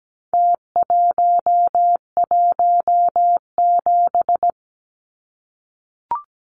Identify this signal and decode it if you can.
Morse code, heard as a keyed tone, about 17 wpm, 700 Hz. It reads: T117
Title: Numbers - Contest Style - 4-Digit 17wpm